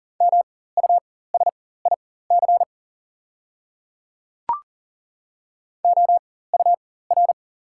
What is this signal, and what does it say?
Morse code, heard as a keyed tone, about 38 wpm, spaced as for 18 wpm, 700 Hz. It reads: MUSIC OUR
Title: 4x Character Spacing - Top 300 Words - 40wpm